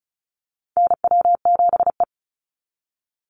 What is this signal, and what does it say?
Morse code, heard as a keyed tone, about 36 wpm, 700 Hz. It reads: NW7E
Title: NA SOTA Chaser Callsigns 35wpm